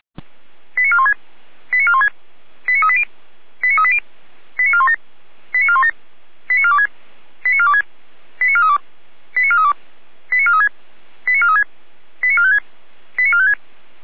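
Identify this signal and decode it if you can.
Five-tone selective call (ZVEI), heard as quick runs of five tones, tones of about 70 ms. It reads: 89416 89416 89380 89380 89417 89417 89426 89426 89432 89432 89436 89436 89456 89456
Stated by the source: ZVEI five-tone call 89361 received by SDR, archived by feuerwehr-orlamuende (Orlamünde fire brigade).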